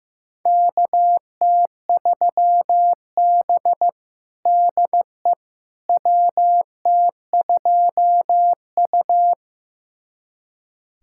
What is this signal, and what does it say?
Morse code, heard as a keyed tone, about 16 wpm, 700 Hz. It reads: KT3B DE WT2U
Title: USA Callsign DE Callsign - 4 Characters - 15wpm